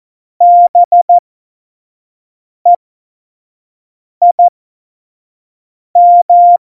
Morse code, heard as a keyed tone, about 14 wpm, spaced as for 5 wpm, 700 Hz. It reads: BEIM
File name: Morsekurs / Witze / Witz 15. September 2020, 25 F70